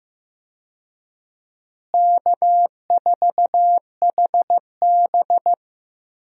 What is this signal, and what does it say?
Morse code, heard as a keyed tone, about 15 wpm, 700 Hz. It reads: K4HB
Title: USA Call Signs - 4 Characters - Repeated 3x - 15wpm